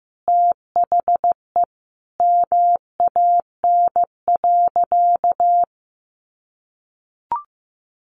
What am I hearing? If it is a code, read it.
Morse code, heard as a keyed tone, about 14 wpm, 700 Hz. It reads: THE MAN.
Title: Sentences from Top 200 Words 15wpm